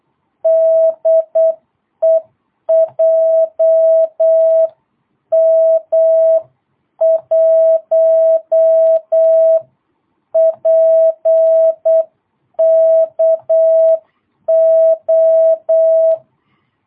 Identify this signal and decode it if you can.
Morse code, heard as a keyed tone, about 8 wpm, 650 Hz. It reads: DE JM1PKO
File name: jh8cbh / 2022cw